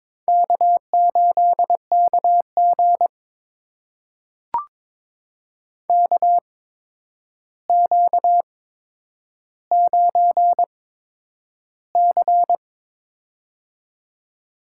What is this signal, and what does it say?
Morse code, heard as a keyed tone, about 22 wpm, spaced as for 9 wpm, 700 Hz. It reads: K8KG KQ9C
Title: USA Call Signs - 4 Characters - 8x Character Spacing - 22wpm